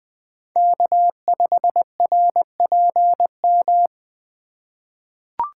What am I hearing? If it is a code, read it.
Morse code, heard as a keyed tone, about 20 wpm, 700 Hz. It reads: K5RPM
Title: USA Call Signs - 5 Characters - 2x Character Spacing - 20wpm